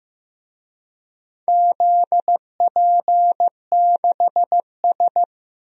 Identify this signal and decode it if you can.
Morse code, heard as a keyed tone, about 15 wpm, 700 Hz. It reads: ZP6S
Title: DXCC-100 Call Signs - 4 Characters - Repeated 3x - 15wpm